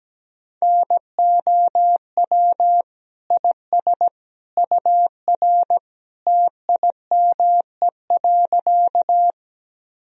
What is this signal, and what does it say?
Morse code, heard as a keyed tone, about 17 wpm, 700 Hz. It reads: NOW IS UR TIME.
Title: Sentences from Top 100 Words and Abbreviations 17wpm